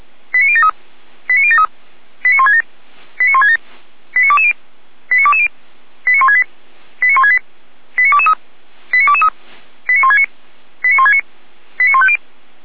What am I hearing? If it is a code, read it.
Five-tone selective call (ZVEI), heard as quick runs of five tones, tones of about 70 ms. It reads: 89073 89073 89167 89167 89220 89220 89168 89168 89223 89223 89169 89169 89150
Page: SDR/ZVEI